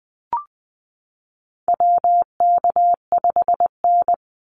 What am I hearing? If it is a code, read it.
Morse code, heard as a keyed tone, about 20 wpm, 700 Hz. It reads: WK5N
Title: Call Signs Repeated 2x - 20wpm